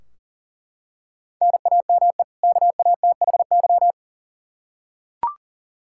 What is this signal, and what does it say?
Morse code, heard as a keyed tone, about 37 wpm, 700 Hz. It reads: NAME KATHY
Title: Practice QSO - Exchange 1 - Beginning - 40wpm